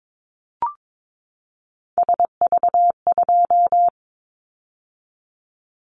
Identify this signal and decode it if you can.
Morse code, heard as a keyed tone, about 22 wpm, 700 Hz. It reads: SV2
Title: Callsign Prefixes for Top 50 DXCC - 22wpm